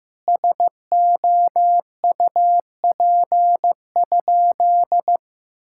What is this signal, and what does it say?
Morse code, heard as a keyed tone, about 15 wpm, 700 Hz. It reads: SOUP?
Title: Sentences from Top 1000 Words - Review Part 6 - 15wpm